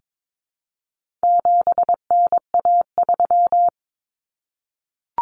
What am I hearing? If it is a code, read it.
Morse code, heard as a keyed tone, about 22 wpm, 700 Hz. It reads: 7NA3
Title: Common Cut Numbers - 4-Digit 22wpm